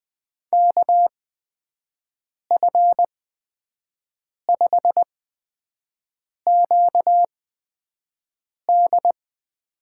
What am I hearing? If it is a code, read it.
Morse code, heard as a keyed tone, about 20 wpm, spaced as for 5 wpm, 700 Hz. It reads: KF5QD
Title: USA Call Signs - 6 Characters - 8x Character Spacing - 20wpm